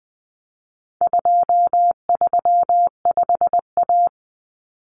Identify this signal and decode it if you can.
Morse code, heard as a keyed tone, about 20 wpm, 700 Hz. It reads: 235A
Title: Common Cut Numbers - 4-Digit 20wpm